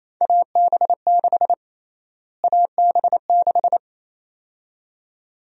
Morse code, heard as a keyed tone, about 28 wpm, 700 Hz. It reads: AB6 AB6
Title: USA Call Sign Prefixes - Repeated 2x - 28wpm